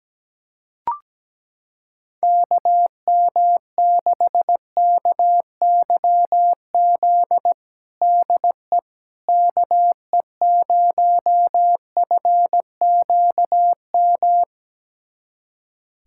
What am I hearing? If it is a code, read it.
Morse code, heard as a keyed tone, about 17 wpm, 700 Hz. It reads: KM6KYZ DE KE0FQM
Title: USA Callsign DE Callsign 17wpm